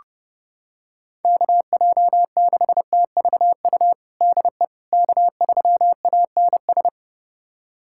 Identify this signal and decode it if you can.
Morse code, heard as a keyed tone, about 31 wpm, 700 Hz. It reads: KJ6TVU DE K3ANS
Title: USA Callsign DE Callsign 30wpm